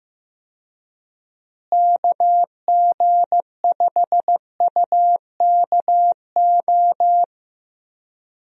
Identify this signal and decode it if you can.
Morse code, heard as a keyed tone, about 15 wpm, 700 Hz. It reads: KG5UKO
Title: USA Call Signs - 6 Characters - Repeated 3x - 15wpm